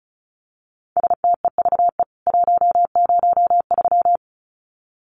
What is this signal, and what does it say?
Morse code, heard as a keyed tone, about 34 wpm, 700 Hz. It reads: STEVE 103
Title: CWT Exchange 35wpm